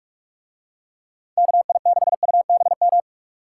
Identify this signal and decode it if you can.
Morse code, heard as a keyed tone, about 42 wpm, 700 Hz. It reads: KI6UBM